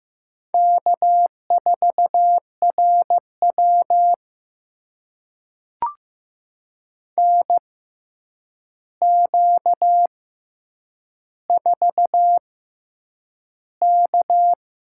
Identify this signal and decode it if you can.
Morse code, heard as a keyed tone, about 15 wpm, spaced as for 8 wpm, 700 Hz. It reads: K4RW NQ4K
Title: USA Call Signs - 4 Characters - 6x Character Spacing - 15wpm